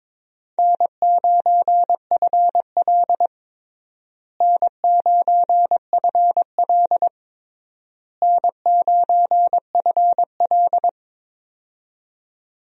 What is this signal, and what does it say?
Morse code, heard as a keyed tone, about 21 wpm, 700 Hz. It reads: N9FL N9FL N9FL
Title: USA Call Signs - 4 Characters - Repeated 3x - 22wpm